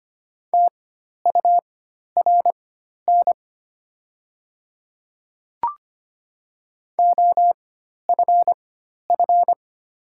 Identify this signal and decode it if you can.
Morse code, heard as a keyed tone, about 25 wpm, spaced as for 12 wpm, 700 Hz. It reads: TURN OFF
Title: 4x Character Spacing - Top 300 Words - 25wpm